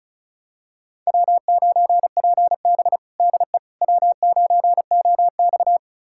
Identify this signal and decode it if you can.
Morse code, heard as a keyed tone, about 36 wpm, 700 Hz. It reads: W9PB DE W9OX
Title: USA Callsign DE Callsign - 4 Characters - 35wpm